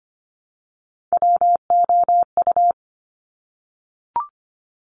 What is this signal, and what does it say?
Morse code, heard as a keyed tone, about 25 wpm, 700 Hz. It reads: WOU
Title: Top Three Letter Combinations in Words 25wpm